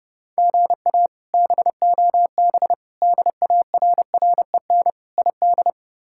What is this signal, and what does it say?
Morse code, heard as a keyed tone, about 32 wpm, 700 Hz. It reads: GA BOB DARREN ID
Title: K1USN SST QSO Practice - 30wpm